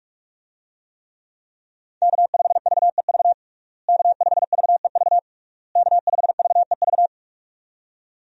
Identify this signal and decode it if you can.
Morse code, heard as a keyed tone, about 46 wpm, 700 Hz. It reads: K5VEV K5VEV K5VEV
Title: USA Call Signs - 5 Characters - Repeated 3x - 45wpm